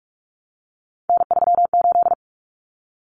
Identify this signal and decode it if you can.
Morse code, heard as a keyed tone, about 44 wpm, 700 Hz. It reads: N38
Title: Common Cut Numbers - 3-Digit 45wpm